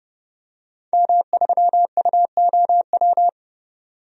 Morse code, heard as a keyed tone, about 30 wpm, 700 Hz. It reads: M3UOW
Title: DXCC-100 Call Signs - 5 Characters - 30wpm